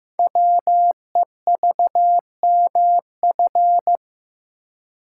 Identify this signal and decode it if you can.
Morse code, heard as a keyed tone, about 15 wpm, 700 Hz. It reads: WEVMF
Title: Code Groups - 5 Letters - 15wpm